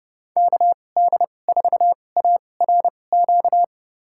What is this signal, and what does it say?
Morse code, heard as a keyed tone, about 30 wpm, spaced as for 22 wpm, 700 Hz. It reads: KD4ARQ
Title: USA Call Signs - 6 Characters - 2x Character Spacing - 30wpm